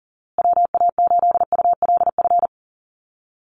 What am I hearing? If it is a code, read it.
Morse code, heard as a keyed tone, about 38 wpm, 700 Hz. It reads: WA8ULF
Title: USA Call Signs Repeated 2x 40wpm